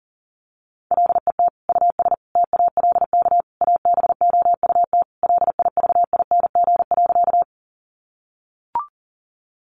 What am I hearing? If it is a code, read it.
Morse code, heard as a keyed tone, about 37 wpm, 700 Hz. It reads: LET US TALK ABOUT LIVING.
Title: Sentences from Top 1100 Words 40wpm